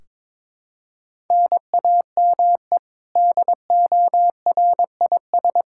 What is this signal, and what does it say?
Morse code, heard as a keyed tone, about 22 wpm, 700 Hz. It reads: NAME DORIS
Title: Practice QSO - Exchange 1 - Beginning - 22wpm